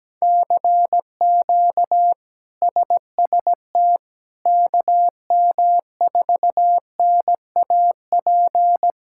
Morse code, heard as a keyed tone, about 17 wpm, 700 Hz. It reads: CQ SST KM4NAP
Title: K1USN SST QSO Practice - 17wpm